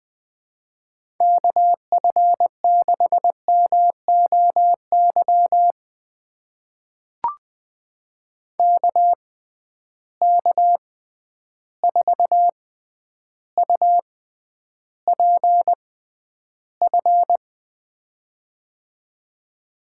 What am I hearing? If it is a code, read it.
Morse code, heard as a keyed tone, about 20 wpm, spaced as for 10 wpm, 700 Hz. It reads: KF6MOY KK4UPF
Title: USA Call Signs - 6 Characters - 6x Character Spacing - 20wpm